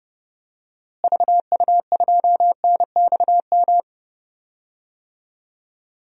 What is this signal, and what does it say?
Morse code, heard as a keyed tone, about 29 wpm, 700 Hz. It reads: VU2NXM
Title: DXCC-100 Call Signs - 6 Characters - Repeated 3x - 30wpm